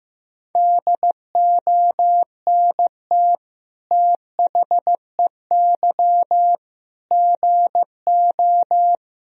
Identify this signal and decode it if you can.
Morse code, heard as a keyed tone, about 15 wpm, 700 Hz. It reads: DONT THEY GO
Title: Sentences from Top 1000 Words - Review Part 1 - 15wpm